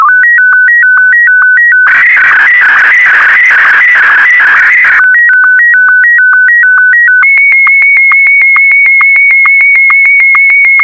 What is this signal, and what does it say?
SSTV (slow-scan television), Martin 1.
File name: sstv / M1-VK4MSL